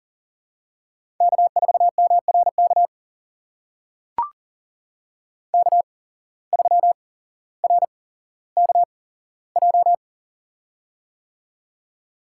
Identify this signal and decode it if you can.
Morse code, heard as a keyed tone, about 40 wpm, spaced as for 17 wpm, 700 Hz. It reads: K4MRK K3RKJ
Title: USA Call Signs - 8x Character Spacing - 40wpm